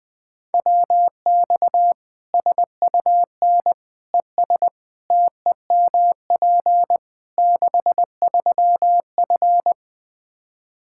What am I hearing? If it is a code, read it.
Morse code, heard as a keyed tone, about 20 wpm, 700 Hz. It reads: WX SUN ES TEMP 63F
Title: Practice QSO - Exchange 2 - Info - 20wpm